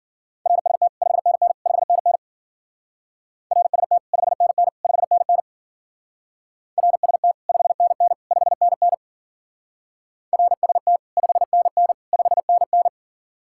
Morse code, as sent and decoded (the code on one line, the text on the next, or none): .-. ... - / ..... -. -. / ..... -. -. / .-. ... - / ..... -. -. / ..... -. -. / .-. ... - / ..... -. -. / ..... -. -. / .-. ... - / ..... -. -. / ..... -. -.
RST 5NN 5NN RST 5NN 5NN RST 5NN 5NN RST 5NN 5NN